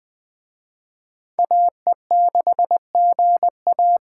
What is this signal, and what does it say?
Morse code, heard as a keyed tone, about 20 wpm, 700 Hz. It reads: AE6GA